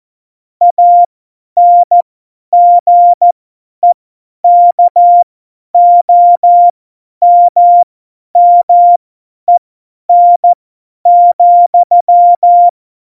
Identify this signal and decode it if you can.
Morse code, heard as a keyed tone, about 14 wpm, spaced as for 10 wpm, 700 Hz. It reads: ANGEKOMMEN,
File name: Morsekurs / Witze / Witz 11 August 2020, 50 F70